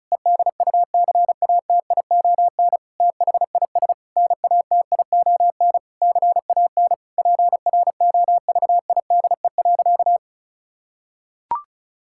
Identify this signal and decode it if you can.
Morse code, heard as a keyed tone, about 36 wpm, 700 Hz. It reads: EDUCATION THIS NATION CAN PROVIDE.
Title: Sentences from Top 1000 Words - Review Part 6 - 35wpm